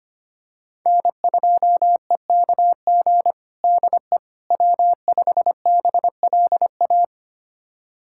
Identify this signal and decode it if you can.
Morse code, heard as a keyed tone, about 25 wpm, 700 Hz. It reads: N2EKG DE W5BLA
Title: USA Callsign DE Callsign - 5 Characters - 25wpm